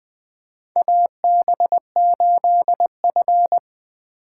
Morse code, heard as a keyed tone, about 21 wpm, 700 Hz. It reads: AB8F
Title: USA Call Signs - 4x Character Spacing - 20wpm